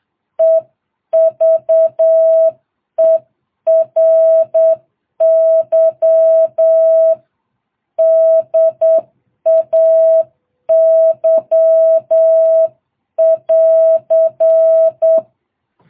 Morse code, heard as a keyed tone, about 8 wpm, 660 Hz. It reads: EVERY DAY <AR>